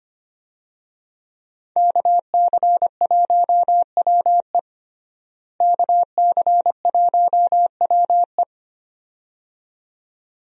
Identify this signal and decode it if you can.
Morse code, heard as a keyed tone, about 25 wpm, 700 Hz. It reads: KC1WE KC1WE
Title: USA Call Signs - 5 Characters - Repeated 2x - 25wpm